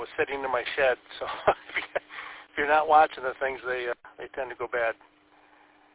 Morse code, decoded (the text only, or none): none